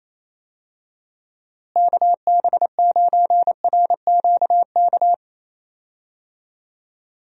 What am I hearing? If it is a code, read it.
Morse code, heard as a keyed tone, about 28 wpm, 700 Hz. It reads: KB9RQK
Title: USA Call Signs - 6 Characters - Repeated 2x - 28wpm